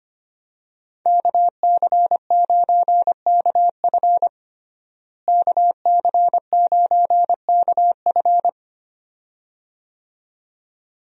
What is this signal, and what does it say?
Morse code, heard as a keyed tone, about 25 wpm, 700 Hz. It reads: KC9KF KC9KF
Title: USA Call Signs - 5 Characters - Repeated 2x - 25wpm